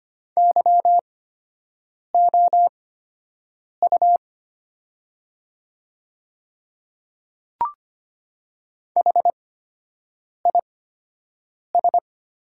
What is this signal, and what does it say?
Morse code, heard as a keyed tone, about 25 wpm, spaced as for 7 wpm, 700 Hz. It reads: YOU HIS
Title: 8x Character Spacing - Top 100 Words - 25wpm